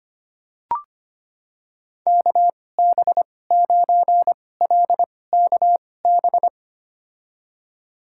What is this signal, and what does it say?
Morse code, heard as a keyed tone, about 25 wpm, spaced as for 18 wpm, 690 Hz. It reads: KB9LKB
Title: USA Call Signs - 6 Characters - 2x Character Spacing - 25wpm